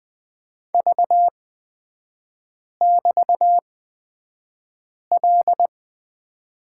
Morse code, heard as a keyed tone, about 20 wpm, spaced as for 5 wpm, 700 Hz. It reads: V<BT>L
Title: 236 Lesson 38e - Letter q - Three Characters 20wpm